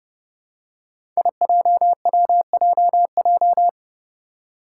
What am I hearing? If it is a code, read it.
Morse code, heard as a keyed tone, about 31 wpm, 700 Hz. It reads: IJWJJ
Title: Five Challenging Letter-Number Combinations 30wpm